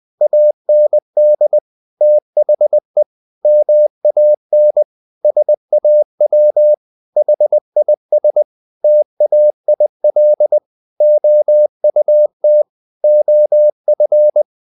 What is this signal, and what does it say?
Morse code, heard as a keyed tone, about 20 wpm, 600 Hz. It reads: AND THE MAN SAW HIS TAIL OUT OF
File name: the old man and the sea part 2 20wpm